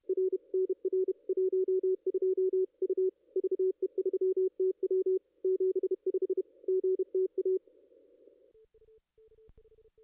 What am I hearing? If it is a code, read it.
Morse code, heard as a keyed tone, about 32 wpm, 380 Hz. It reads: R NR 12 U VE3TW 75 GTA
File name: contests / ss-cw-k5tr-op-n5rz -2200